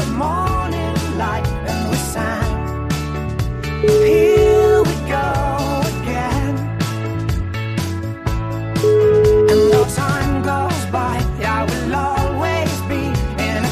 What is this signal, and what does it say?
Ringback tone (Europe, 425 Hz), with noise.